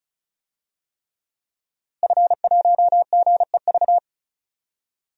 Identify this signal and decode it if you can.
Morse code, heard as a keyed tone, about 36 wpm, 700 Hz. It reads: F1GEV
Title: DXCC-100 Call Signs - 5 Characters - Repeated 3x - 35wpm